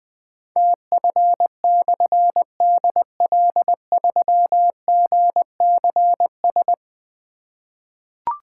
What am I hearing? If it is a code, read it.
Morse code, heard as a keyed tone, about 20 wpm, 700 Hz. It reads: TF/DL3GCS
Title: Difficult Call Signs 20wpm